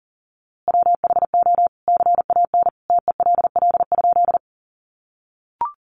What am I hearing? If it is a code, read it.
Morse code, heard as a keyed tone, about 40 wpm, 700 Hz. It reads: WHO CAN TELL?
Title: A Sound Of Thunder 40wpm